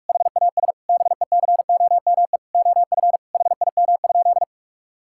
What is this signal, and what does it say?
Morse code, heard as a keyed tone, about 41 wpm, 700 Hz. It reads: HAS BECOME OF HIM?
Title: Sentences from Top 1000 Words - Review Part 5 - 45wpm